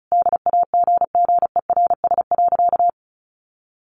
Morse code, heard as a keyed tone, about 34 wpm, 700 Hz. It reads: DAGGERS.
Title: A Sound Of Thunder 35wpm